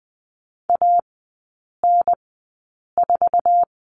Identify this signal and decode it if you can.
Morse code, heard as a keyed tone, about 20 wpm, spaced as for 8 wpm, 700 Hz. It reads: AN4
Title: 107 Lesson 18e - Letter m - Three Characters 20wpm